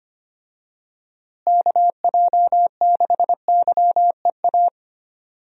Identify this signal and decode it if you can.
Morse code, heard as a keyed tone, about 25 wpm, 700 Hz. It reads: KJ6YEA
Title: USA Call Signs - 6 Characters - Repeated 2x - 25wpm